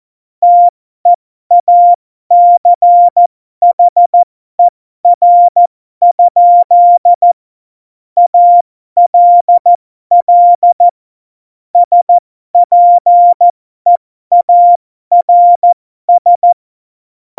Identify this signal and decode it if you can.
Morse code, heard as a keyed tone, about 14 wpm, spaced as for 12 wpm, 700 Hz. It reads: TEACHER? ALL SPEARS
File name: Morsekurs / Witze